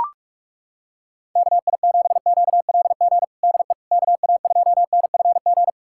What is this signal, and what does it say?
Morse code, heard as a keyed tone, about 41 wpm, 700 Hz. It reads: KI7QLG DE KA2NFG
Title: USA Callsign DE Callsign - 6 Characters - 45wpm